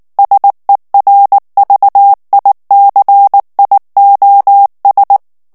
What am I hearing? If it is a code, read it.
Morse code, heard as a keyed tone, about 19 wpm, 800 Hz. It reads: SERVICIOS
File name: morse mp3 / ESPAGNOL TEXTE EN CLAIR 2 VITESSE 18 MOTS MINUTES